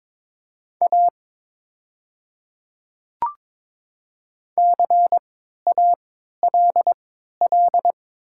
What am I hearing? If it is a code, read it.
Morse code, heard as a keyed tone, about 22 wpm, spaced as for 12 wpm, 690 Hz. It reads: A CALL
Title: 3x Character Spacing - Top 200 Words - 22wpm